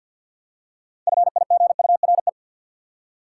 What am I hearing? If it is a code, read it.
Morse code, heard as a keyed tone, about 53 wpm, 700 Hz. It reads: FIGURE
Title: Six Letter Words - 50wpm